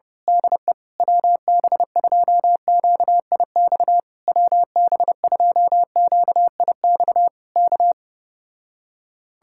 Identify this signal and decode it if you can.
Morse code, heard as a keyed tone, about 29 wpm, 700 Hz. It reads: DE WB2QIX WB2QIX K
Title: Speed-Racing Common QSO Phrases Encore - 25wpm